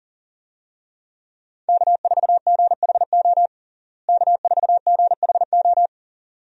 Morse code, heard as a keyed tone, about 37 wpm, 700 Hz. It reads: K4GHO K4GHO
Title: USA Call Signs - 5 Characters - Repeated 2x - 40wpm